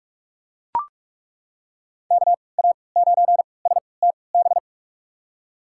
Morse code, heard as a keyed tone, about 42 wpm, spaced as for 25 wpm, 700 Hz. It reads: KA9STB
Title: USA Call Signs - 3x Character Spacing - 45wpm